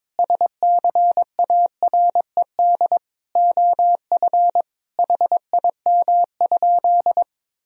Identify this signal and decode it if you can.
Morse code, heard as a keyed tone, about 22 wpm, 700 Hz. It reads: SCARED OF HIM?
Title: Sentences from Top 1000 Words - Review Part 4 - 22wpm